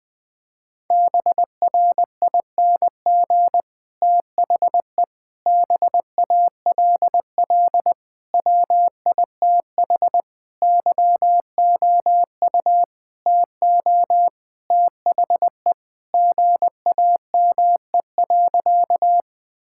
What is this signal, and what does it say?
Morse code, heard as a keyed tone, about 19 wpm, 700 Hz. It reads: BRING THE BALL WITH YOU TO THE GAME.
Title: Sentences from Top 1000 Words - Review Part 6 - 20wpm